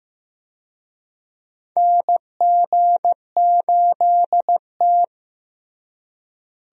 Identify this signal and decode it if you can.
Morse code, heard as a keyed tone, about 15 wpm, 700 Hz. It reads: NG8T
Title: USA Call Signs - 4 Characters - Repeated 2x - 15wpm